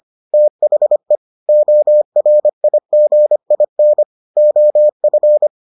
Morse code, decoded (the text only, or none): THE ORIGIN OF